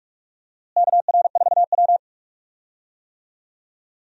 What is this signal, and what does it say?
Morse code, heard as a keyed tone, about 41 wpm, 700 Hz. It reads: KR4W
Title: USA Call Signs - 4 Characters - Repeated 3x - 45wpm